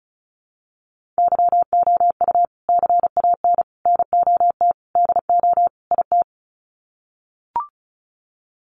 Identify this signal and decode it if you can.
Morse code, heard as a keyed tone, about 34 wpm, 700 Hz. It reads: YOU CAN NOT DO IT
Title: Sets of 5 Words Encore2 35wpm